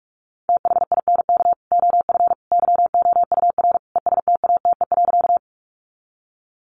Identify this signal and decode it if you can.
Morse code, heard as a keyed tone, about 44 wpm, 700 Hz. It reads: THINK OF YOUR ESTATE.
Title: Sentences from Top 1200 Words 45wpm